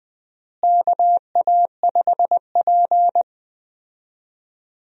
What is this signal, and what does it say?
Morse code, heard as a keyed tone, about 20 wpm, 700 Hz. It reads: KA5P
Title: USA Call Signs - 4 Characters - Repeated 2x - 20wpm